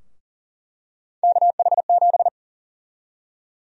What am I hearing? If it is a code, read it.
Morse code, heard as a keyed tone, about 40 wpm, 700 Hz. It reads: KH7
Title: USA Call Sign Prefixes - Repeated 2x - 40wpm